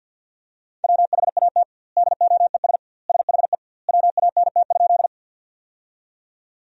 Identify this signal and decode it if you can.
Morse code, heard as a keyed tone, about 46 wpm, 700 Hz. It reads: WHAT DOES SHE WANT?
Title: 2x Word-Spacing - Sentences from Top 200 Words - 50wpm